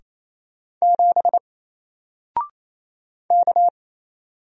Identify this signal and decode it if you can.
Morse code, heard as a keyed tone, about 28 wpm, 700 Hz. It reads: 7 K
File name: Single-Letter-Number-28wpm